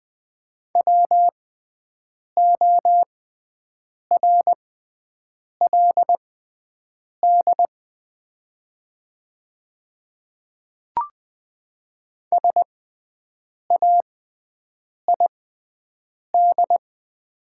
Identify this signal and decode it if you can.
Morse code, heard as a keyed tone, about 20 wpm, spaced as for 7 wpm, 700 Hz. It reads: WORLD SAID
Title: 6x Character Spacing - Top 300 Words - 20wpm